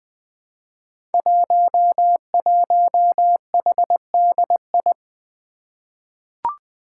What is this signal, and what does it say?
Morse code, heard as a keyed tone, about 20 wpm, 700 Hz. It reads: 11HDI
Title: Five Challenging Letter-Number Combinations 20wpm